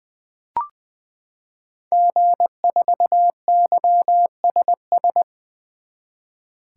Morse code, heard as a keyed tone, about 20 wpm, 700 Hz. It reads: G4YSS
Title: 2023 SOTA Activator Call Signs - 20wpm